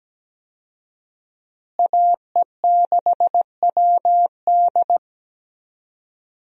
Morse code, heard as a keyed tone, about 17 wpm, 700 Hz. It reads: AE6WD
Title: USA Call Signs - 5 Characters - Repeated 3x - 17wpm